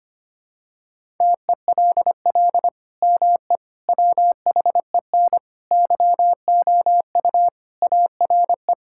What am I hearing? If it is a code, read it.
Morse code, heard as a keyed tone, about 25 wpm, 700 Hz. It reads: TELL ME WHEN YOU ARE
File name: practice / 25wpm / Sentences with 2-Word Phrases - 25wpm